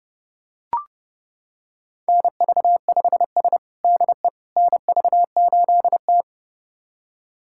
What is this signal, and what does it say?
Morse code, heard as a keyed tone, about 30 wpm, 700 Hz. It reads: NV5S DE NV8T
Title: USA Callsign DE Callsign - 4 Characters - 30wpm